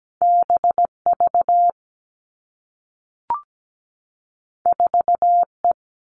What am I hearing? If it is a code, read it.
Morse code, heard as a keyed tone, about 17 wpm, 700 Hz. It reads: B V 4 E
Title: Two Challenging Letter-Number Combinations 17wpm